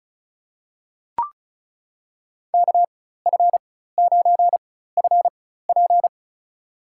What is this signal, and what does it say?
Morse code, heard as a keyed tone, about 36 wpm, spaced as for 16 wpm, 700 Hz. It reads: KF9FP
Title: USA Call Signs - 4x Character Spacing - 35wpm